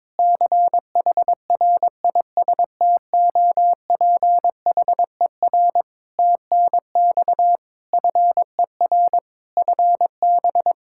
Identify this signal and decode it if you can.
Morse code, heard as a keyed tone, about 22 wpm, 700 Hz. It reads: CHRISTOPHER TNX FER FB
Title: Practice QSO - Exchange 4 - Ending - 22wpm